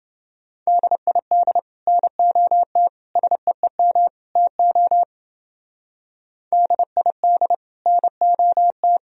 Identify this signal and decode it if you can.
Morse code, heard as a keyed tone, about 30 wpm, 700 Hz. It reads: DID NOT SEEM TO DID NOT